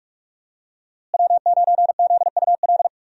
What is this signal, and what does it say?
Morse code, heard as a keyed tone, about 45 wpm, 700 Hz. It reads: W9ZUL
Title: USA Call Signs - 5 Characters - Repeated 3x - 45wpm